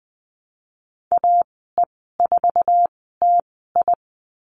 Morse code, heard as a keyed tone, about 19 wpm, spaced as for 14 wpm, 700 Hz. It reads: AE4TI
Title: 055 Lesson 10g - Letter r - Call Signs 20wpm